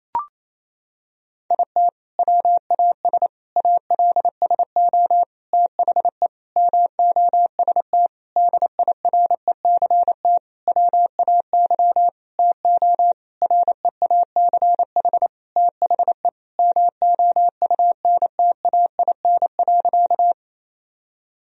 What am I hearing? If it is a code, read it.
Morse code, heard as a keyed tone, about 28 wpm, 700 Hz. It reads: IT WAS ALSO THE MOST DIRECT WAY TO REACH THE MOUNTAIN.
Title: Sentences from Top 1000 Words - Review Part 2 - 28wpm